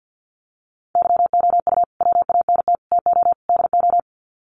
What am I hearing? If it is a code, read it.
Morse code, heard as a keyed tone, about 50 wpm, 700 Hz. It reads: YOU WANT TO DO
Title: Sets of 4 Words 50wpm